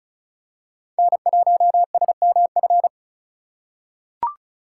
Morse code, heard as a keyed tone, about 36 wpm, 700 Hz. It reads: N1SMF
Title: USA Call Signs - 5 Characters - 2x Character Spacing - 35wpm